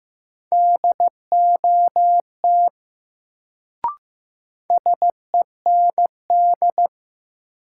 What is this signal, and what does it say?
Morse code, heard as a keyed tone, about 15 wpm, 700 Hz. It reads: DOT SEND
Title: Sentences from Top 1500 Words - New Word Drill Alt - 15wpm